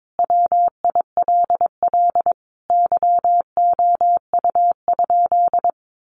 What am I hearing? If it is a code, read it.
Morse code, heard as a keyed tone, about 22 wpm, 700 Hz. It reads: WILL YOU?
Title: Sentences from Top 1100 Words 22wpm